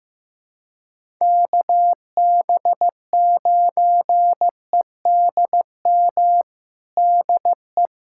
Morse code, heard as a keyed tone, about 15 wpm, 700 Hz. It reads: KB9EDM DE
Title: USA Callsign DE Callsign 15wpm